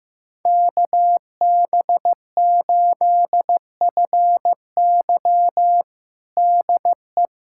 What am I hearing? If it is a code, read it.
Morse code, heard as a keyed tone, about 15 wpm, 700 Hz. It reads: KB8FY DE
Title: USA Callsign DE Callsign 15wpm